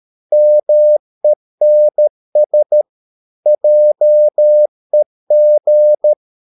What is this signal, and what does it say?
Morse code, heard as a keyed tone, about 13 wpm, 600 Hz. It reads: MENS JEG
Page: Ferie 13 wpm